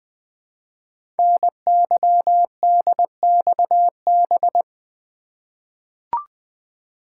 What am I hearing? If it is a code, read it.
Morse code, heard as a keyed tone, about 20 wpm, 700 Hz. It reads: NYDXB
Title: Code Groups - 5 Letters - 20wpm